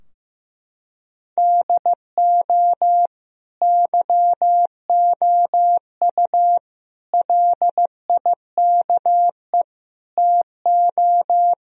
Morse code, heard as a keyed tone, about 15 wpm, 700 Hz. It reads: DO YOU LIKE TO
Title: Sentences from Top 1000 Words - Review Part 3 - 15wpm